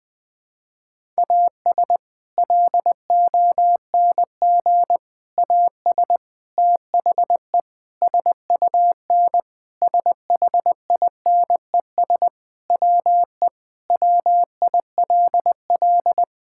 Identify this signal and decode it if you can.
Morse code, heard as a keyed tone, about 20 wpm, 700 Hz. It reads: AS LONG AS THE SUN SHINES WE WILL
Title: Sentences with 4-Word Phrases - 20wpm